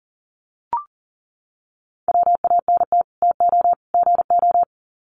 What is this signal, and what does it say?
Morse code, heard as a keyed tone, about 38 wpm, 700 Hz. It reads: WANT TO GO
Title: Sets of 3 Words 40wpm